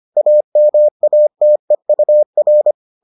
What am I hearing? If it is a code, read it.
Morse code, heard as a keyed tone, about 25 wpm, 600 Hz. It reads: AMATEUR